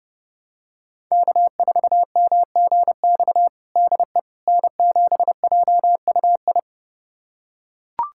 Morse code, heard as a keyed tone, about 32 wpm, 700 Hz. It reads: K4MGX DE N7JUI
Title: USA Callsign DE Callsign - 5 Characters - 30wpm